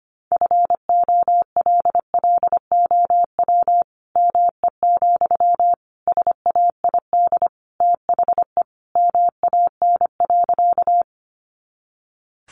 Morse code, heard as a keyed tone, about 25 wpm, 700 Hz. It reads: FOLLOW ME, SAID THE MAN.
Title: Sentences from Top 200 Words 25wpm